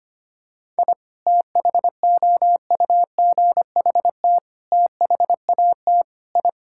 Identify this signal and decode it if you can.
Morse code, heard as a keyed tone, about 25 wpm, 700 Hz. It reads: I THOUGHT THAT I
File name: practice / 25wpm / Sentences with 3-Word Phrases - 25wpm